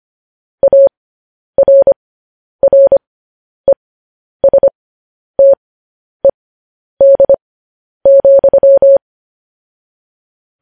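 Morse code, heard as a keyed tone, about 25 wpm, spaced as for 10 wpm, 550 Hz. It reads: ARRESTED,
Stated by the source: Morse news headlines at 10.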